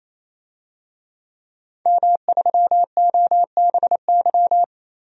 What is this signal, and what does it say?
Morse code, heard as a keyed tone, about 28 wpm, 700 Hz. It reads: M3OBY
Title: DXCC-100 Call Signs - 5 Characters - 28wpm